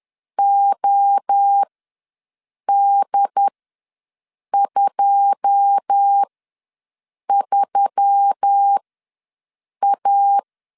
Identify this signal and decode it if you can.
Morse code, heard as a keyed tone, about 10 wpm, spaced as for 6 wpm, 800 Hz. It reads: OD23A
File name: LEZIONI / 9 2 ws 10wpm 4130